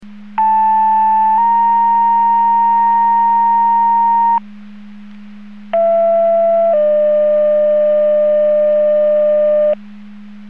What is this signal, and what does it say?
Two-tone page, heard as a one-second tone, then a three-second tone, about 880 then 930 Hz, about 670 then 600 Hz. It is heard from 0.4 to 9.7 s.